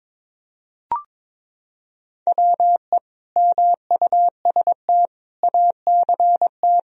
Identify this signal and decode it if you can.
Morse code, heard as a keyed tone, about 22 wpm, 700 Hz. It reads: WE MUST ACT
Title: Sentences from Top 1000 Words - Review Part 1 - 22wpm